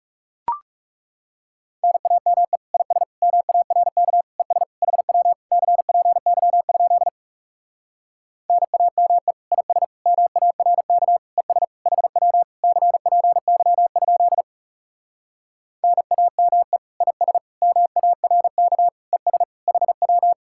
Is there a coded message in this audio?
Yes: Morse code from 1.8 to 20.3 s, heard as a keyed tone, about 40 wpm, 700 Hz. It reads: NAME IS MARK ES HW CPY? NAME IS MARK ES HW CPY? NAME IS MARK ES HW